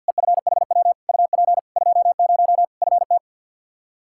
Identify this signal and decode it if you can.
Morse code, heard as a keyed tone, about 52 wpm, 700 Hz. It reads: EFHW UP 20 FT
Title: K1USN SST QSO Practice 50wpm